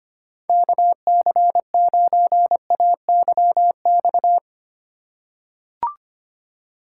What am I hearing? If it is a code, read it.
Morse code, heard as a keyed tone, about 25 wpm, 700 Hz. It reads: KC9AYX
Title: USA Call Signs - 6 Characters - 3x Character Spacing - 25wpm